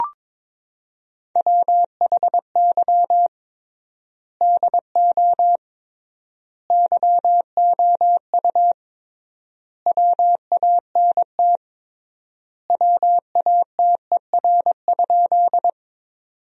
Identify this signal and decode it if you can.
Morse code, heard as a keyed tone, about 22 wpm, 700 Hz. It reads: WHY DO YOU WANT WATER?
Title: 3x Word-Spacing - Sentences from Top 200 Words - 22wpm